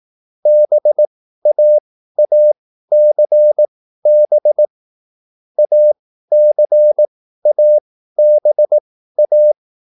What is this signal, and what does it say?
Morse code, heard as a keyed tone, about 18 wpm, spaced as for 13 wpm, 600 Hz. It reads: BAACB ACABA
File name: QSL / CW lesson